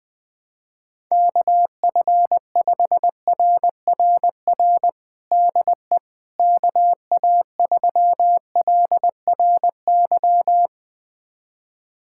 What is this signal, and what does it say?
Morse code, heard as a keyed tone, about 21 wpm, 700 Hz. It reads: KF5RRR DE KA3LRY
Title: USA Callsign DE Callsign - 6 Characters - 20wpm